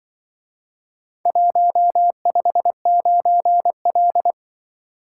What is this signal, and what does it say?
Morse code, heard as a keyed tone, about 25 wpm, 700 Hz. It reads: 159L
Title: Speed-Racing - Four Letter-Number Combinations - 20wpm